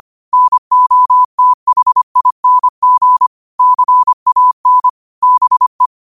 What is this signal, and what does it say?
Morse code, heard as a keyed tone, about 25 wpm, 1000 Hz. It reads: NOTHING CAN BE
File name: qotdfiles / 25wpm